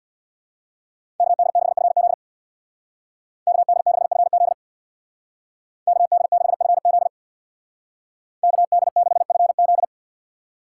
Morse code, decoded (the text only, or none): KD6FZ KD6FZ KD6FZ KD6FZ